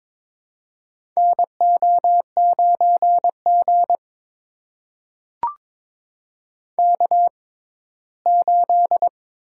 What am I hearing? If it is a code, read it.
Morse code, heard as a keyed tone, about 21 wpm, spaced as for 15 wpm, 700 Hz. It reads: NO9G K8
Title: USA Call Signs - 4 Characters - 6x Character Spacing - 22wpm